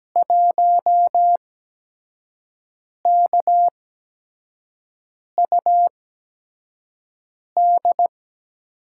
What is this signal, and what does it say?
Morse code, heard as a keyed tone, about 17 wpm, spaced as for 5 wpm, 700 Hz. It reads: 1KUD
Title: USA Call Signs - 5 Characters - 8x Character Spacing - 17wpm